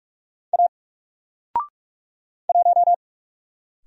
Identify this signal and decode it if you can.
Morse code, heard as a keyed tone, about 44 wpm, 700 Hz. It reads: A 1
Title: Single Letter-Number 45wpm